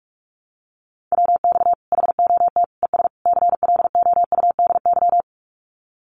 Morse code, heard as a keyed tone, about 44 wpm, 700 Hz. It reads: WX HOT ES CLOUDY
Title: Common QSO Phrases 45wpm